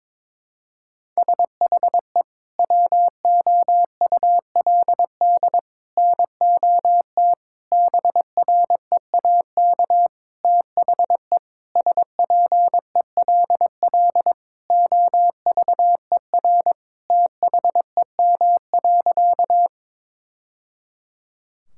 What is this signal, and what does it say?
Morse code, heard as a keyed tone, about 22 wpm, 700 Hz. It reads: SHE WOULD NOT BREAK THE SPELL OVER THEM.
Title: Sentences from Top 1000 Words - Review Part 6 - 22wpm